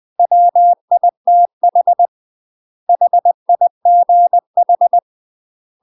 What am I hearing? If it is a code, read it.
Morse code, heard as a keyed tone, about 20 wpm, 700 Hz. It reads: WITH HIGH
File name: UK MASTER QSOs TRAINING / 20WPM / UK MASTER QSOs 0005